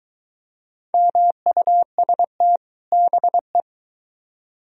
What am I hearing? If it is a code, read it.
Morse code, heard as a keyed tone, about 22 wpm, 700 Hz. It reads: MUST BE